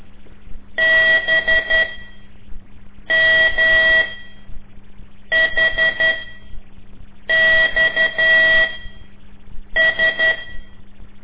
Morse code, heard as a keyed tone, about 11 wpm, spaced as for 5 wpm, 1950 Hz. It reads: BMHXS
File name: ik1weg2 / CORSI CW / lez5-1